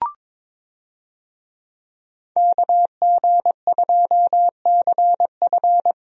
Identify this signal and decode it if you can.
Morse code, heard as a keyed tone, about 21 wpm, 700 Hz. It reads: KG2CF
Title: USA Call Signs - 5 Characters - Repeated 3x - 22wpm